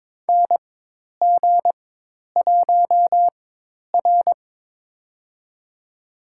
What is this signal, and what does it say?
Morse code, heard as a keyed tone, about 22 wpm, spaced as for 10 wpm, 700 Hz. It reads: NG1R
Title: USA Call Signs - 4x Character Spacing - 22wpm